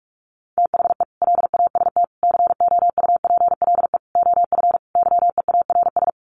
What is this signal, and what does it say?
Morse code, heard as a keyed tone, about 46 wpm, 700 Hz. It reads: THE LAST COUPLE OF YEARS
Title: Sets of 5 Words 45wpm